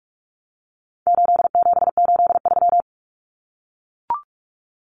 Morse code, heard as a keyed tone, about 44 wpm, 700 Hz. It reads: 8783
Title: Common Cut Numbers - 4-Digit 45wpm